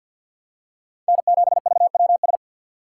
Morse code, heard as a keyed tone, about 53 wpm, 700 Hz. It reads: N7VWS
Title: USA Call Signs - 5 Characters - 2x Character Spacing - 50wpm